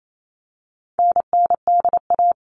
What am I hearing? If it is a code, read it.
Morse code, heard as a keyed tone, about 28 wpm, 700 Hz. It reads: NNDA
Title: All Cut Numbers - 4-Digit 28wpm